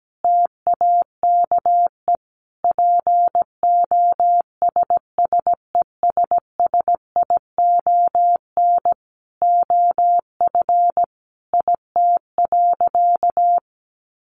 Morse code, heard as a keyed tone, about 17 wpm, 700 Hz. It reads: TAKE POSSESSION OF IT.
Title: Sentences from Top 1400 Words 17wpm